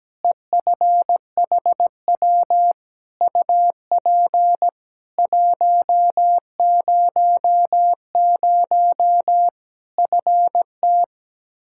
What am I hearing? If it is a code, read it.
Morse code, heard as a keyed tone, about 17 wpm, 700 Hz. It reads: EFHW UP 100 FT
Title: K1USN SST QSO Practice 17wpm